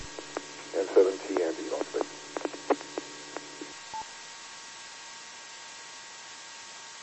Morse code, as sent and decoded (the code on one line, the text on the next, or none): none